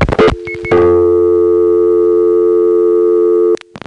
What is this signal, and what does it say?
Dial tone (North America), 350 + 440 Hz.
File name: phonetrips / SmyrnaGA3HQ